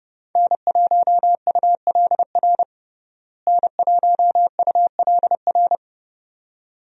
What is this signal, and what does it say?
Morse code, heard as a keyed tone, about 30 wpm, 700 Hz. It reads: N1ULR N1ULR
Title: USA Call Signs - 5 Characters - Repeated 2x - 30wpm